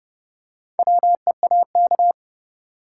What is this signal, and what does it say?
Morse code, heard as a keyed tone, about 30 wpm, 700 Hz. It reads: WEAK
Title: Four Letter Words - 30wpm